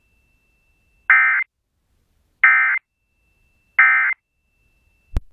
Emergency Alert System (SAME) tones, heard as harsh buzzing bursts. It is heard from 1.1 to 4.1 s.